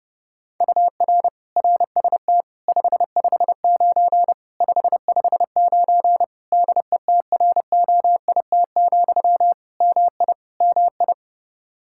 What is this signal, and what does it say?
Morse code, heard as a keyed tone, about 30 wpm, 700 Hz. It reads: UR RST 559 559 DETROIT, MI MI
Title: Practice QSO - Exchange 1 - Beginning - 30wpm